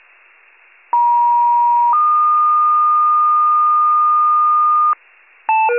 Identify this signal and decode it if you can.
Two-tone page, heard as a one-second tone, then a three-second tone, about 950 then 1220 Hz, with noise. It is heard from 0.9 to 4.9 s.